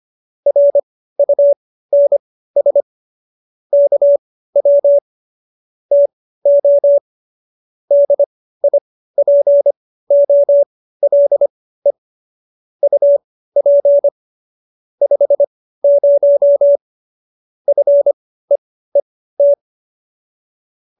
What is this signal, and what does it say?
Morse code, heard as a keyed tone, about 25 wpm, spaced as for 15 wpm, 570 Hz. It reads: RUNS KW TO DIPOLE UP 50 FEET